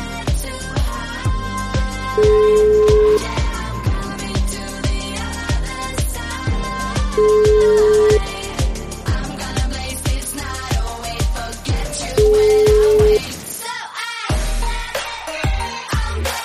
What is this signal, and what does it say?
Ringback tone (Europe, 425 Hz), with noise.